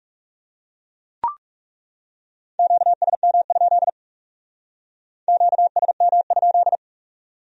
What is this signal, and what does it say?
Morse code, heard as a keyed tone, about 42 wpm, 700 Hz. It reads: QSM? QSM?